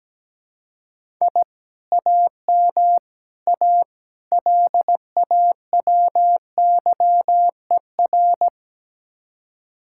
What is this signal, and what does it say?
Morse code, heard as a keyed tone, about 18 wpm, 700 Hz. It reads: I AM A LAWYER
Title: Speed-Racing Common QSO Phrases - 17wpm